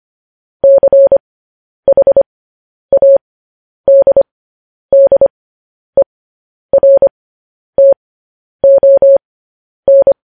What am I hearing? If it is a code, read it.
Morse code, heard as a keyed tone, about 25 wpm, spaced as for 10 wpm, 550 Hz. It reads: CHADDERTON
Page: news-headlines-in-morse-code-at-10-wpm